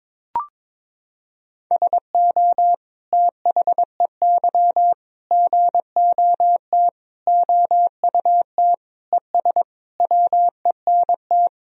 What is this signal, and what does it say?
Morse code, heard as a keyed tone, about 22 wpm, 700 Hz. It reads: SO THEY GOT OUT ES WENT
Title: Sentences from Top 300 Words and Abbreviations 22wpm